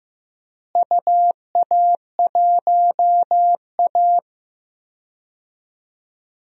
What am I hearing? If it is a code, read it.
Morse code, heard as a keyed tone, about 16 wpm, 700 Hz. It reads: UA1A
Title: DXCC-100 Call Signs - 4 Characters - Repeated 3x - 15wpm